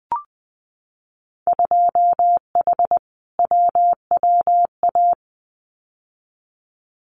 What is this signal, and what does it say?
Morse code, heard as a keyed tone, about 20 wpm, 700 Hz. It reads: 2H WWA
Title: Winter Field Day Exchanges 20wpm